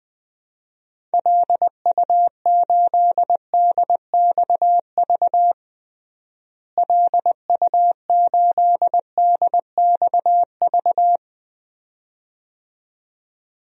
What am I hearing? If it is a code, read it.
Morse code, heard as a keyed tone, about 20 wpm, 700 Hz. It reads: LU8DXV LU8DXV
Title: DXCC-100 Call Signs - 6 Characters - Repeated 2x - 20wpm